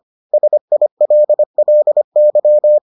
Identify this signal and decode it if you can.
Morse code, heard as a keyed tone, about 25 wpm, 600 Hz. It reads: SILLY